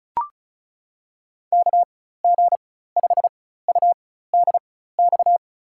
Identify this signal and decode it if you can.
Morse code, heard as a keyed tone, about 34 wpm, spaced as for 16 wpm, 700 Hz. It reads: KG5UDX
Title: USA Call Signs - 6 Characters - 4x Character Spacing - 35wpm